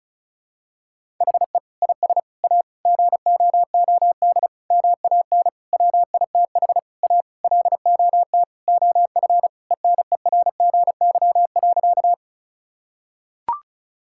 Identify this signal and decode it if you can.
Morse code, heard as a keyed tone, about 36 wpm, 700 Hz. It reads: HE IS A GOOD MAN WITH A LOT OF ENERGY.
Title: Sentences from Top 1000 Words - Review Part 6 - 35wpm